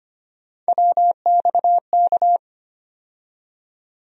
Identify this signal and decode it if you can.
Morse code, heard as a keyed tone, about 25 wpm, 700 Hz. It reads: WXK
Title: USA Call Sign Postfixes - 3 Letters 25wpm